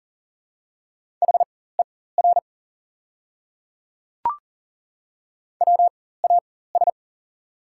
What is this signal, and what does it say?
Morse code, heard as a keyed tone, about 40 wpm, spaced as for 19 wpm, 690 Hz. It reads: HER WAS
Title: 4x Character Spacing - Top 100 Words - 40wpm